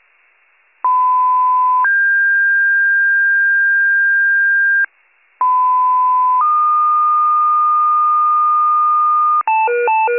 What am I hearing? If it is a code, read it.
Two-tone page, heard as a one-second tone, then a three-second tone, about 1010 then 1600 Hz, about 1010 then 1220 Hz, with noise. It is heard from 0.8 to 9.4 s.